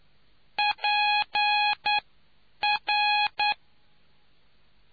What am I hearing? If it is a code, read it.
Morse code, heard as a keyed tone, about 9 wpm, spaced as for 8 wpm, 830 Hz. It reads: PR